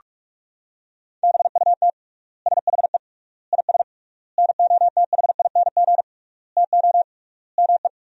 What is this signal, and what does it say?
Morse code, heard as a keyed tone, about 42 wpm, 700 Hz. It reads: BUT SHE IS NOTHING TO ME
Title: 3x Word-Spacing - Sentences from Top 500 Words - 45wpm